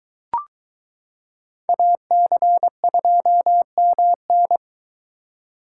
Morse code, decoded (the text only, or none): AC2MN